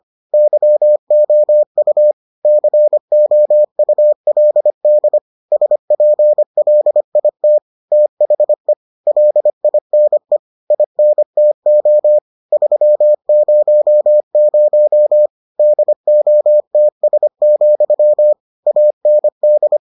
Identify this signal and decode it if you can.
Morse code, heard as a keyed tone, about 25 wpm, 600 Hz. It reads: YOU COULD SPLIT THE LINE INTO 300 DOTS, AND